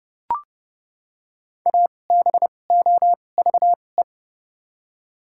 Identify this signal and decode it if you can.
Morse code, heard as a keyed tone, about 30 wpm, spaced as for 22 wpm, 700 Hz. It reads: ABOVE
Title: 2x Character Spacing - Top 500 Words - 30wpm